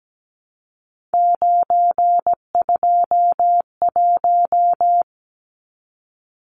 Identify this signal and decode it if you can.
Morse code, heard as a keyed tone, about 17 wpm, 700 Hz. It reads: 921
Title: Numbers - 3-Digit 17wpm